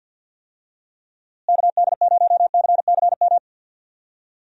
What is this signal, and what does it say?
Morse code, heard as a keyed tone, about 47 wpm, 700 Hz. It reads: KD0CCM
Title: USA Call Signs - 6 Characters - Repeated 2x - 50wpm